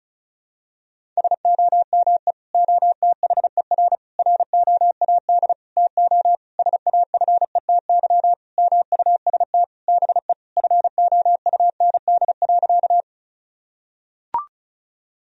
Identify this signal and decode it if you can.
Morse code, heard as a keyed tone, about 34 wpm, 700 Hz. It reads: SOME OTHER ROAD TO SAFETY MUST BE FOUND.
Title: Sentences from Top 1000 Words - Review Part 4 - 35wpm